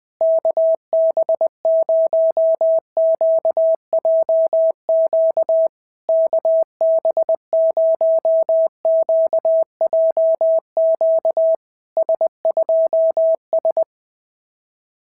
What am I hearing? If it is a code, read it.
Morse code, heard as a keyed tone, about 20 wpm, 650 Hz. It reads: KB0QJQ KB0QJQ S2S
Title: SOTA QSO Practice 20wpm